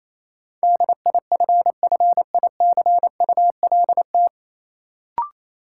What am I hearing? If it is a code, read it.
Morse code, heard as a keyed tone, about 28 wpm, 700 Hz. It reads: DIFFICULT
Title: Sets of 4 Words - New Word Drill - 28wpm